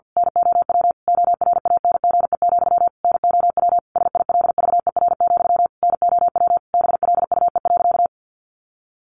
Morse code, heard as a keyed tone, about 50 wpm, 700 Hz. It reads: NOW ORANGE, NOW SILVER, NOW BLUE.
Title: A Sound Of Thunder 50wpm